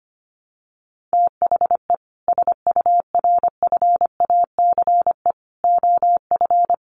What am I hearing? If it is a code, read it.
Morse code, heard as a keyed tone, about 25 wpm, 700 Hz. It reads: THE SURFACE OF
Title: Sets of 4 Words 25wpm